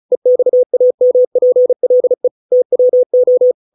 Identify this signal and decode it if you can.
Morse code, heard as a keyed tone, about 34 wpm, 500 Hz. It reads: EXAMPLE TWO